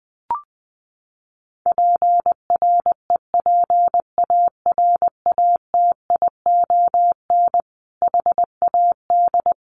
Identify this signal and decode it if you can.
Morse code, heard as a keyed tone, about 20 wpm, 700 Hz. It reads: PREPARATION HAD
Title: Sentences from Top 1400 Words 20wpm